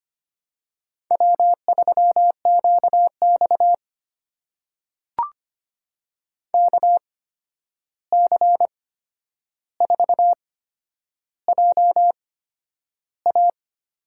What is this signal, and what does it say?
Morse code, heard as a keyed tone, about 25 wpm, spaced as for 10 wpm, 700 Hz. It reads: W3QX KC4JA
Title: USA Call Signs - 8x Character Spacing - 25wpm